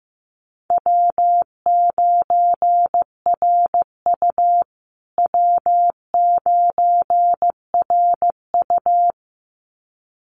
Morse code, heard as a keyed tone, about 15 wpm, 700 Hz. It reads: W9RU W9RU
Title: USA Call Signs Repeated 2x 15wpm